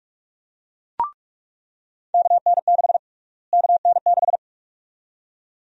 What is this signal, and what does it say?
Morse code, heard as a keyed tone, about 46 wpm, 700 Hz. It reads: KN6 KN6
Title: USA Call Sign Prefixes - Repeated 2x - 45wpm